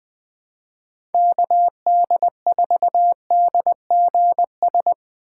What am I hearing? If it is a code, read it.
Morse code, heard as a keyed tone, about 21 wpm, 700 Hz. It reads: KD4DGS
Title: USA Call Signs - 6 Characters - 4x Character Spacing - 20wpm